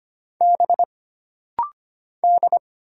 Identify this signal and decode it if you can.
Morse code, heard as a keyed tone, about 25 wpm, 700 Hz. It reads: B D
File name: Single Character (Letter:Number:Punctuation:Prosign) - Rapid-Fire - 25wpm